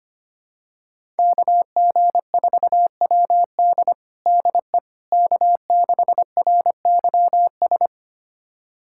Morse code, heard as a keyed tone, about 25 wpm, 700 Hz. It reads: KG4WD DE K6RYS
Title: USA Callsign DE Callsign - 5 Characters - 25wpm